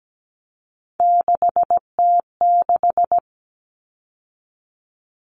Morse code, heard as a keyed tone, about 17 wpm, 700 Hz. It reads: 6T6
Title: All Cut Numbers - 3-Digit 17wpm